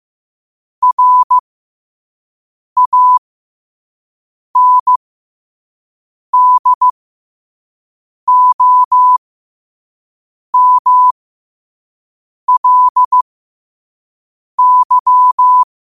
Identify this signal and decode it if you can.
Morse code, heard as a keyed tone, about 15 wpm, spaced as for 5 wpm, 1000 Hz. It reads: RANDOMLY
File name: qotdfiles / 5wpm